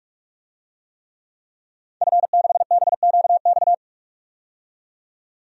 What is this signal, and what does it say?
Morse code, heard as a keyed tone, about 41 wpm, 700 Hz. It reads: F6BQX
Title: DXCC-100 Call Signs - 5 Characters - 45wpm